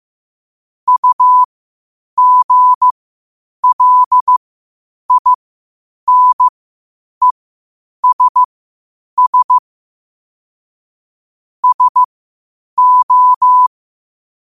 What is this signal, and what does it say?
Morse code, heard as a keyed tone, about 15 wpm, spaced as for 8 wpm, 1000 Hz. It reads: UGLINESS SO
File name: qotdfiles / 8wpm